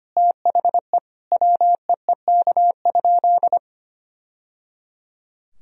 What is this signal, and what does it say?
Morse code, heard as a keyed tone, about 25 wpm, 700 Hz. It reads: THE WEEK?
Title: Sentences from Top 1000 Words - Review Part 2 - 25wpm